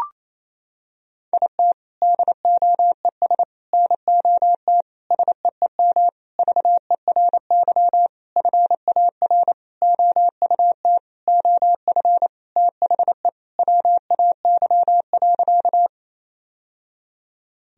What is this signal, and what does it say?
Morse code, heard as a keyed tone, about 28 wpm, 700 Hz. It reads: IT DOES NOT SEEM VERY FAR OUT OF THE WAY.
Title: Sentences from Top 1000 Words - Review Part 1 - 28wpm